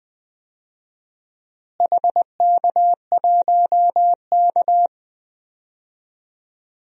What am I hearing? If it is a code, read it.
Morse code, heard as a keyed tone, about 21 wpm, 700 Hz. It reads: HK1K
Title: DXCC-100 Call Signs - 4 Characters - Repeated 2x - 20wpm